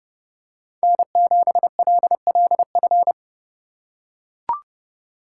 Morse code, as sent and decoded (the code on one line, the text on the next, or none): -. --... .-.. .-.. ..-.
N7LLF